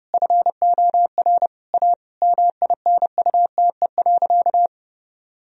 Morse code, heard as a keyed tone, about 30 wpm, 700 Hz. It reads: FOR A MINUTE.
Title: Sentences from Top 1000 Words - Review Part 2 - 30wpm